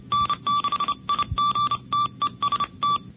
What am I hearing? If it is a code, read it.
Morse code, heard as a keyed tone, about 31 wpm, 1160 Hz. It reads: N6IG TEST
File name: NCCC sweepstakes